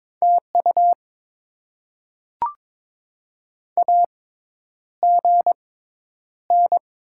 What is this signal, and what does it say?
Morse code, heard as a keyed tone, about 22 wpm, spaced as for 10 wpm, 700 Hz. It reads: TU AGN
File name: Top 20 QSO Elements - 6x Character Spacing - 22wpm